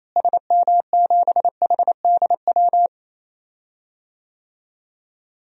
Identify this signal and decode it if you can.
Morse code, heard as a keyed tone, about 28 wpm, 700 Hz. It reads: SM7HDW
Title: DXCC-100 Call Signs - 6 Characters - Repeated 2x - 28wpm